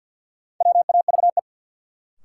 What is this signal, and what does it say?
Morse code, heard as a keyed tone, about 50 wpm, 700 Hz. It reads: WAVE
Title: Sentences from Top 1400 Words - New Word Drill - 50wpm